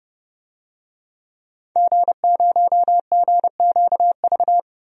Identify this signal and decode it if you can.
Morse code, heard as a keyed tone, about 30 wpm, 700 Hz. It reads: G0GQV
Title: DXCC-100 Call Signs - 5 Characters - 30wpm